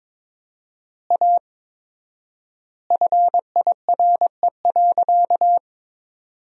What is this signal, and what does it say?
Morse code, heard as a keyed tone, about 22 wpm, 700 Hz. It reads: A FIRE.
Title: 4x Word-Spacing - Sentences from Top 400 Words - 22wpm